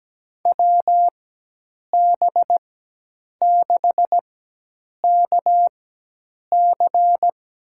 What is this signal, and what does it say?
Morse code, heard as a keyed tone, about 17 wpm, spaced as for 8 wpm, 700 Hz. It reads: WB6KC
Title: USA Call Signs - 6 Characters - 4x Character Spacing - 17wpm